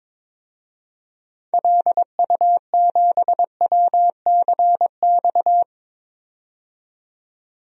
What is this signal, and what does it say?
Morse code, heard as a keyed tone, about 21 wpm, 700 Hz. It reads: LU7WCX
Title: DXCC-100 Call Signs - 6 Characters - 22wpm